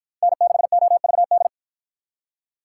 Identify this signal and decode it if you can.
Morse code, heard as a keyed tone, about 53 wpm, 700 Hz. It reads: N6OVD